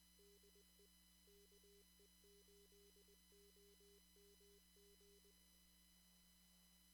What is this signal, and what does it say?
Morse code, heard as a keyed tone, about 21 wpm, 400 Hz. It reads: DE KE8OMG
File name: CW / kt5j wpx cw-op-k5pi